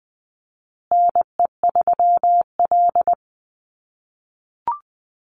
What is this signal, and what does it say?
Morse code, heard as a keyed tone, about 20 wpm, 700 Hz. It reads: NE3L
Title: USA Call Signs Repeated 3x 20wpm